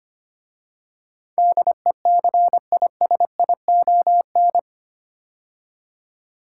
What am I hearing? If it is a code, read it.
Morse code, heard as a keyed tone, about 25 wpm, 690 Hz. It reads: DECISION
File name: 8 Letter Words - First 3 Letters Given - 25wpm